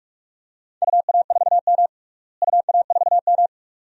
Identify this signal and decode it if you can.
Morse code, heard as a keyed tone, about 43 wpm, 700 Hz. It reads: UA4M UA4M
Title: DXCC-100 Call Signs - 4 Characters - Repeated 2x - 45wpm